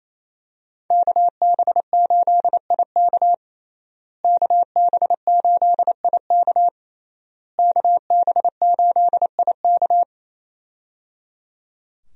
Morse code, heard as a keyed tone, about 27 wpm, 700 Hz. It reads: KB8IK KB8IK KB8IK
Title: USA Call Signs - 5 Characters - Repeated 3x - 28wpm